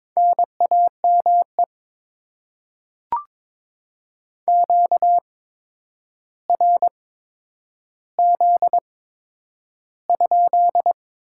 Morse code, heard as a keyed tone, about 22 wpm, spaced as for 9 wpm, 700 Hz. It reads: NAME QRZ?
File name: Top 90 QSO Elements - 8x Character Spacing - 22wpm